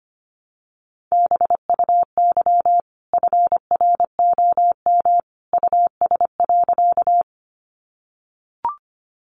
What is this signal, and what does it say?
Morse code, heard as a keyed tone, about 25 wpm, 700 Hz. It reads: BUY FROM US.
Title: Sentences from Top 700 Words 25wpm